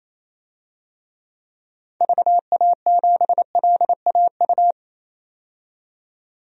Morse code, heard as a keyed tone, about 28 wpm, 700 Hz. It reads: VA7LAU
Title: DXCC-100 Call Signs - 6 Characters - 28wpm